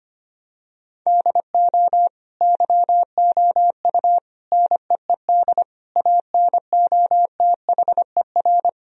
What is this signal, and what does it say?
Morse code, heard as a keyed tone, about 25 wpm, 700 Hz. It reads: DO YOU NEED ANOTHER
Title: Sentences from Top 1000 Words - Review Part 4 - 25wpm